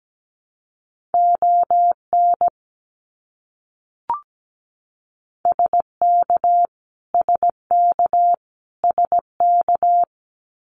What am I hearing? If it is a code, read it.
Morse code, heard as a keyed tone, about 17 wpm, 700 Hz. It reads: ON SK SK SK
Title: Canadian Provinces and Territories - Repeated 3x - 17wpm